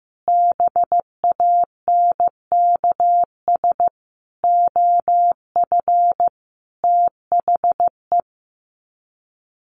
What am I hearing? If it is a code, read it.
Morse code, heard as a keyed tone, about 15 wpm, 700 Hz. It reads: BANKS OF THE
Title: Sets of 5 Words Encore2 15wpm